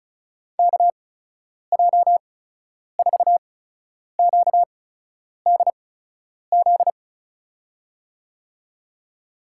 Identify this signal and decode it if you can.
Morse code, heard as a keyed tone, about 36 wpm, spaced as for 10 wpm, 700 Hz. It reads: KJ4QDZ
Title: USA Call Signs - 6 Characters - 8x Character Spacing - 35wpm